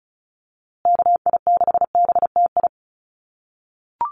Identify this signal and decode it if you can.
Morse code, heard as a keyed tone, about 34 wpm, 700 Hz. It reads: KI6BTI
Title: USA Call Signs Repeated 2x 35wpm